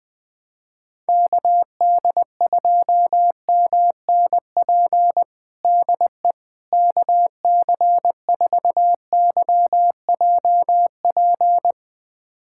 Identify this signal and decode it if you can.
Morse code, heard as a keyed tone, about 20 wpm, 700 Hz. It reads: KD2MNP DE KC4YJP
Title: USA Callsign DE Callsign 20wpm